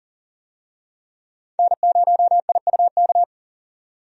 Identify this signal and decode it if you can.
Morse code, heard as a keyed tone, about 40 wpm, 700 Hz. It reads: N0IUK
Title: USA Call Signs - 5 Characters - 40wpm